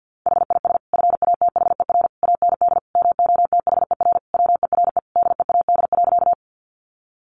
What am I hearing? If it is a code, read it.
Morse code, heard as a keyed tone, about 50 wpm, 700 Hz. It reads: HIS FATHER AND MOTHER WERE DEAD.
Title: Sentences from Top 1500 Words 50wpm